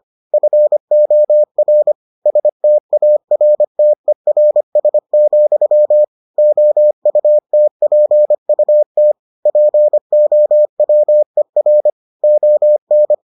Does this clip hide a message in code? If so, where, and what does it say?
Yes: Morse code from 0.3 to 13.1 s, heard as a keyed tone, about 25 wpm, 600 Hz. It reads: FOR STARTERS, OUTPUT POWER ON